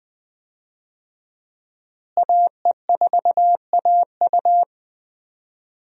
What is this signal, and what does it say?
Morse code, heard as a keyed tone, about 20 wpm, 700 Hz. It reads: AE4AU
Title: USA Call Signs - 5 Characters - Repeated 2x - 20wpm